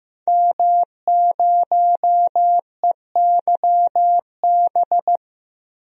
Morse code, heard as a keyed tone, about 15 wpm, 700 Hz. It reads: M0EYB
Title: DXCC-100 Call Signs - 5 Characters - 15wpm